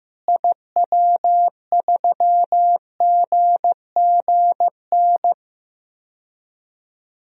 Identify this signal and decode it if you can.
Morse code, heard as a keyed tone, about 15 wpm, 700 Hz. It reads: IW3GGN
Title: DXCC-100 Call Signs - 6 Characters - Repeated 2x - 15wpm